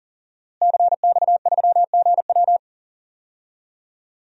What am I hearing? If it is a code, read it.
Morse code, heard as a keyed tone, about 37 wpm, 700 Hz. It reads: CX3GW
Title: DXCC-100 Call Signs - 5 Characters - Repeated 2x - 40wpm